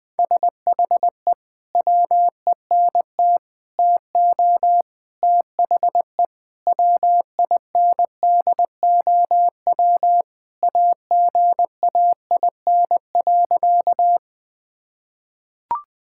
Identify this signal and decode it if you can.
Morse code, heard as a keyed tone, about 21 wpm, 700 Hz. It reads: SHE WENT TO THE WINDOW AGAIN.
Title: Sentences from Top 1600 Words 20wpm